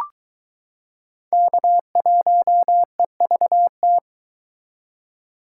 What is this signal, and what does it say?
Morse code, heard as a keyed tone, about 21 wpm, 700 Hz. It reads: K1EVT